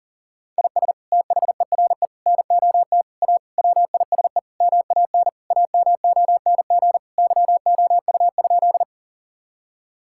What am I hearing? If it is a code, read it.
Morse code, heard as a keyed tone, about 40 wpm, 700 Hz. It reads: IS THERE NOT A WISE MAN AMONG YOU?
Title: Sentences from Top 1600 Words 40wpm